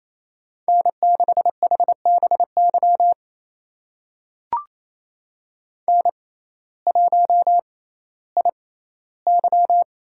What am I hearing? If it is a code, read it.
Morse code, heard as a keyed tone, about 28 wpm, spaced as for 15 wpm, 700 Hz. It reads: N6HBY N1IY
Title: USA Call Signs - 6x Character Spacing - 28wpm